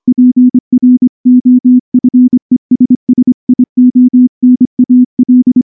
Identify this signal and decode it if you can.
Morse code, heard as a keyed tone, about 24 wpm, 250 Hz. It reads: PROFESSIONAL
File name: morse-training-25WPM